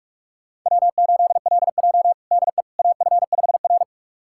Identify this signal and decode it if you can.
Morse code, heard as a keyed tone, about 46 wpm, 700 Hz. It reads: W8LJ DE AF5R
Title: USA Callsign DE Callsign - 4 Characters - 45wpm